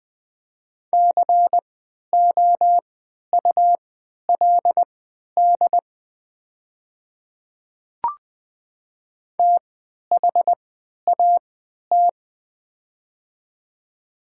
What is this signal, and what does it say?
Morse code, heard as a keyed tone, about 20 wpm, spaced as for 11 wpm, 690 Hz. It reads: COULD THAT
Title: 3x Character Spacing - Top 100 Words - 20wpm